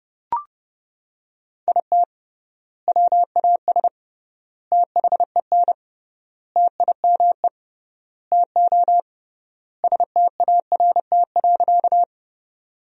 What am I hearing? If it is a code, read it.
Morse code, heard as a keyed tone, about 30 wpm, 700 Hz. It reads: IT WAS THEN TIME TO START.
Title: 3x Word-Spacing - Sentences from Top 300 Words - 30wpm